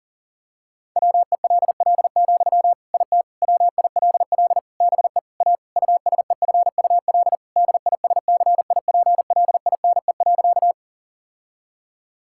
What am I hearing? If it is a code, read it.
Morse code, heard as a keyed tone, about 40 wpm, 700 Hz. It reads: WELL, IT WILL BE A USEFUL DISCIPLINE.
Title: Sentences from Top 1000 Words - Review Part 6 - 40wpm